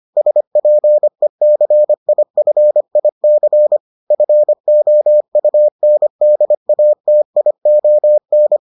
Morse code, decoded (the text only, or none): SPECIFIC FOUNDATION